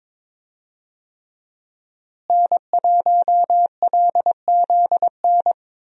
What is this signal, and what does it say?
Morse code, heard as a keyed tone, about 22 wpm, 700 Hz. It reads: N1LZN
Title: USA Call Signs - 5 Characters - Repeated 3x - 22wpm